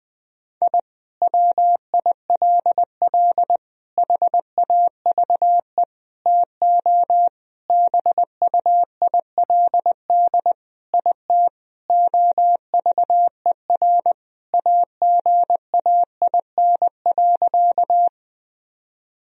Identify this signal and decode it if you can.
Morse code, heard as a keyed tone, about 21 wpm, 700 Hz. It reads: I WILL HAVE TO BUILD IT OVER AGAIN.
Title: Sentences from Top 1000 Words - Review Part 1 - 20wpm